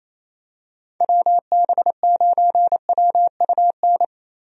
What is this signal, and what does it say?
Morse code, heard as a keyed tone, about 27 wpm, 700 Hz. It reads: WB9WUN
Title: USA Call Signs - 6 Characters - Repeated 2x - 28wpm